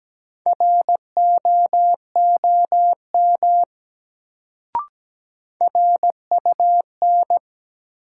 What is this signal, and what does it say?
Morse code, heard as a keyed tone, about 17 wpm, 700 Hz. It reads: ROOM RUN
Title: Sets of 4 Words - New Word Drill - 17wpm